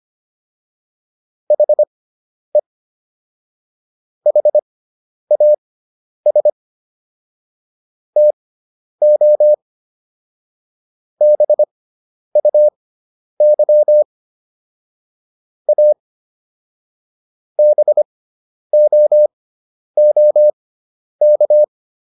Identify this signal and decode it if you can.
Morse code, heard as a keyed tone, about 25 wpm, spaced as for 10 wpm, 600 Hz. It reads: HE HAS TO BUY A BOOK